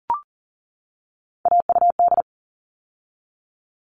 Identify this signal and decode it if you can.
Morse code, heard as a keyed tone, about 39 wpm, 700 Hz. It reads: AUD
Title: All Cut Numbers - 3-Digit 40wpm